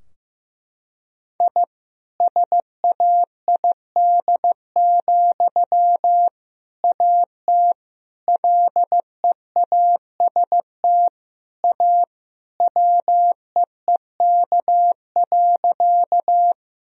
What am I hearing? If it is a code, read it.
Morse code, heard as a keyed tone, about 16 wpm, 700 Hz. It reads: I SAID, AT LEAST A WEEK.
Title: Sentences from Top 1000 Words - Review Part 2 - 15wpm